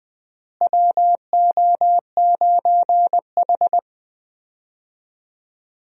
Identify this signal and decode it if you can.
Morse code, heard as a keyed tone, about 20 wpm, 700 Hz. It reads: WO9H
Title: USA Call Signs - 4 Characters - Repeated 3x - 20wpm